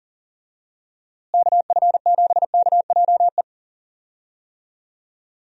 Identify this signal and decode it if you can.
Morse code, heard as a keyed tone, about 37 wpm, 700 Hz. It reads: KF7KJE
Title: USA Call Signs - 6 Characters - Repeated 2x - 40wpm